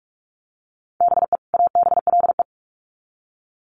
Morse code, heard as a keyed tone, about 44 wpm, 700 Hz. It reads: BE ABLE
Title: Sets of 2 Words 45wpm